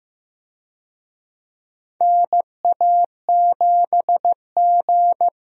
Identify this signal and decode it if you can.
Morse code, heard as a keyed tone, about 15 wpm, 700 Hz. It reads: NA7G
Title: USA Call Signs - 4 Characters - Repeated 3x - 15wpm